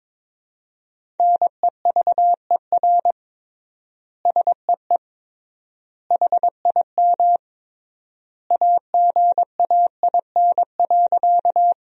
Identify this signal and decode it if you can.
Morse code, heard as a keyed tone, about 22 wpm, 700 Hz. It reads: NEVER SEE HIM AGAIN.
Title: 3x Word-Spacing - Sentences from Top 300 Words - 22wpm